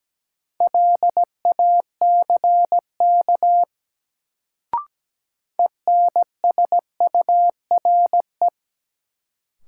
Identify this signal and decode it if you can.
Morse code, heard as a keyed tone, about 17 wpm, 700 Hz. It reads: LACK ENSURE
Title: Sentences from Top 1100 Words - New Word Drill - 17wpm